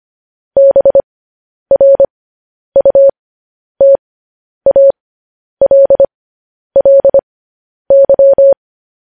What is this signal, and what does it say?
Morse code, heard as a keyed tone, about 25 wpm, spaced as for 10 wpm, 550 Hz. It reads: BRUTALLY
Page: news headlines in morse code at 10 wpm